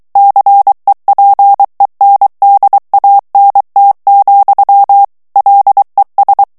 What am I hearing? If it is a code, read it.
Morse code, heard as a keyed tone, about 23 wpm, 800 Hz. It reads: CEPENDANT, LES
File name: morse mp3 / FRANCAIS TEXTE CLAIR 1 VITESSE 22 MOTS MINUTES